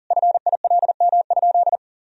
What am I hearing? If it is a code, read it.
Morse code, heard as a keyed tone, about 40 wpm, 700 Hz. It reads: FILM?
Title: Sentences from Top 1000 Words - Review Part 3 - 40wpm